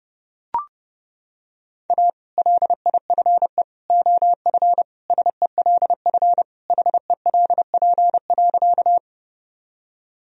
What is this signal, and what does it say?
Morse code, heard as a keyed tone, about 32 wpm, 700 Hz. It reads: A LIFE OF SELF HELP.
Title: Sentences from Top 1000 Words - Review Part 1 - 30wpm